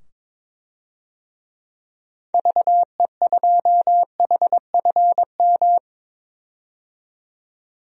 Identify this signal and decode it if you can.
Morse code, heard as a keyed tone, about 22 wpm, 700 Hz. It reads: VE2HFM
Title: DXCC-100 Call Signs - 6 Characters - 22wpm